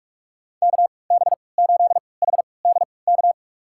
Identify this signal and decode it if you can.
Morse code, heard as a keyed tone, about 46 wpm, spaced as for 26 wpm, 700 Hz. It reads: KB8HDK
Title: USA Call Signs - 6 Characters - 3x Character Spacing - 45wpm